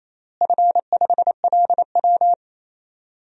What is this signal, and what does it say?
Morse code, heard as a keyed tone, about 28 wpm, 700 Hz. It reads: F5LW
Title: DXCC-100 Call Signs - 4 Characters - Repeated 3x - 28wpm